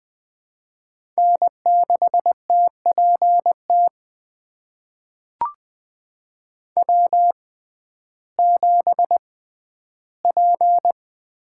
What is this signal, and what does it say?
Morse code, heard as a keyed tone, about 21 wpm, spaced as for 12 wpm, 700 Hz. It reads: N6TPT W7P
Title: USA Call Signs - 6x Character Spacing - 20wpm